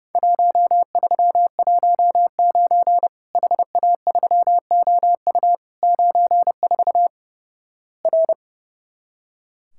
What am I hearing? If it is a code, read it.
Morse code, heard as a keyed tone, about 32 wpm, 700 Hz. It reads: 1319 HA3OU 94 R
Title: Worked All Europe DX Contest - QTC Exchange - 30wpm